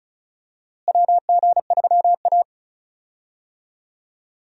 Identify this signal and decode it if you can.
Morse code, heard as a keyed tone, about 34 wpm, 700 Hz. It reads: WG3A
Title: USA Call Signs - 4 Characters - Repeated 3x - 35wpm